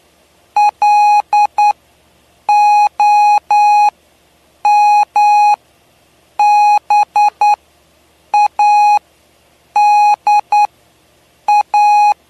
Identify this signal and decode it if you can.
Morse code, heard as a keyed tone, about 9 wpm, spaced as for 7 wpm, 830 Hz. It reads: LOMBADA
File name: cw / py2lcd / 07-PY2LCD-Li��o 7